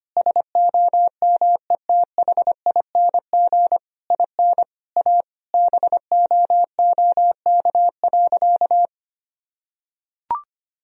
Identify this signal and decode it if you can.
Morse code, heard as a keyed tone, about 25 wpm, 700 Hz. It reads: SOMETHING IN A BOOK.
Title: Sentences from Top 1000 Words - Review Part 1 - 25wpm